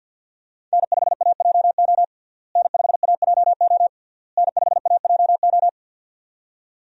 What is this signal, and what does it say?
Morse code, heard as a keyed tone, about 51 wpm, 700 Hz. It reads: N5AJO N5AJO N5AJO
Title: USA Call Signs - 5 Characters - Repeated 3x - 50wpm